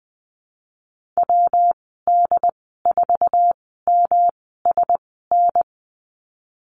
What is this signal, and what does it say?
Morse code, heard as a keyed tone, about 20 wpm, spaced as for 15 wpm, 700 Hz. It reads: WD4MSN
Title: 147 Lesson 24g - Letter y - Call Signs 20wpm